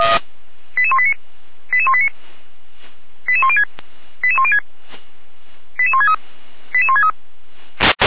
Five-tone selective call (ZVEI), heard as quick runs of five tones, tones of about 80 ms. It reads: 88189 88189 88106 88106 88163 88163